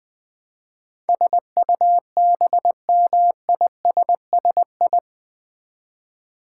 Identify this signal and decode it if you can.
Morse code, heard as a keyed tone, about 20 wpm, 700 Hz. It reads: SUBMISSI
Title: Build A Word - 4 Letters At A Time - 10-Letter Words - 20wpm